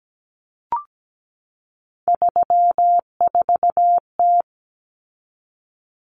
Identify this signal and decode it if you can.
Morse code, heard as a keyed tone, about 17 wpm, 700 Hz. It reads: 34T
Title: Common Cut Numbers - 3-Digit 17wpm